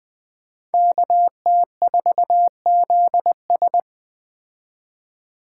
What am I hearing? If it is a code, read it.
Morse code, heard as a keyed tone, about 20 wpm, 700 Hz. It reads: KT4ZS
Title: Speed Racing - USA 2x2 Call Signs 17wpm